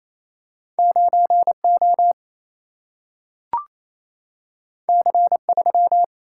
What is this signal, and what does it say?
Morse code, heard as a keyed tone, about 28 wpm, 700 Hz. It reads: 9 O C 3
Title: Single-Letter Single-Number 28wpm